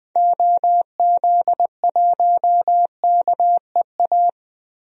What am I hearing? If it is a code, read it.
Morse code, heard as a keyed tone, about 20 wpm, 700 Hz. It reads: OZ1KEA
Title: DXCC-100 Call Signs - 6 Characters - Repeated 2x - 20wpm